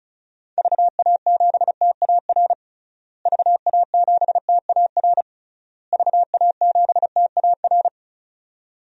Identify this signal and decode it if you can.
Morse code, heard as a keyed tone, about 36 wpm, 700 Hz. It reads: VA7TAR VA7TAR VA7TAR
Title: DXCC-100 Call Signs - 6 Characters - Repeated 3x - 35wpm